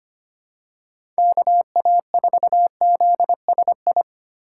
Morse code, heard as a keyed tone, about 25 wpm, 700 Hz. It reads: KA4ZSI
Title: USA Call Signs - 6 Characters - 6x Character Spacing - 25wpm